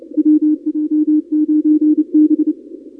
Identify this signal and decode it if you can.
Morse code, heard as a keyed tone, about 30 wpm, 310 Hz. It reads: WJ9B
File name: contests / 2019-cw-wpx-kt5j-k5pi- -2330 0-31